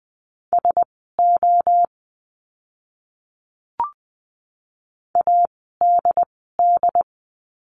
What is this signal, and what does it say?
Morse code, heard as a keyed tone, about 20 wpm, spaced as for 15 wpm, 700 Hz. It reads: SO ADD
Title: 075 Lesson 13f - Letter L - Words 20wpm